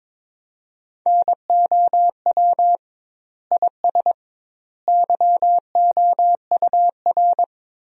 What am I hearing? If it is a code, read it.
Morse code, heard as a keyed tone, about 22 wpm, 700 Hz. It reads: NOW IS YOUR
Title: 2x Word-Spacing - Sentences from Top 100 Words - 22wpm